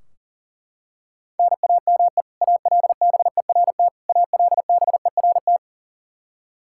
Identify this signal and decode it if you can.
Morse code, heard as a keyed tone, about 37 wpm, 700 Hz. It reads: NAME ALBERT ALBERT
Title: Practice QSO - Exchange 1 - Beginning - 40wpm